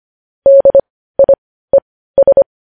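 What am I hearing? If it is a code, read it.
Morse code, heard as a keyed tone, about 25 wpm, spaced as for 15 wpm, 550 Hz. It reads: DIES
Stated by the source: Morse code news headlines sent at 15, preceded by vvv.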